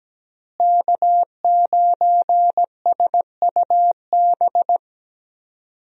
Morse code, heard as a keyed tone, about 17 wpm, 700 Hz. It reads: K9SUB
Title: USA Call Signs - 5 Characters - Repeated 2x - 17wpm